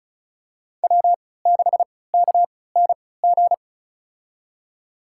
Morse code, heard as a keyed tone, about 36 wpm, spaced as for 20 wpm, 700 Hz. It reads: W6KNG
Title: USA Call Signs - 3x Character Spacing - 35wpm